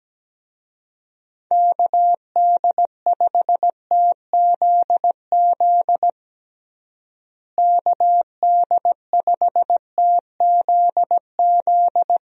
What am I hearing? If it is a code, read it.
Morse code, heard as a keyed tone, about 17 wpm, 700 Hz. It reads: KD5TZZ KD5TZZ
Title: USA Call Signs - 6 Characters - Repeated 2x - 17wpm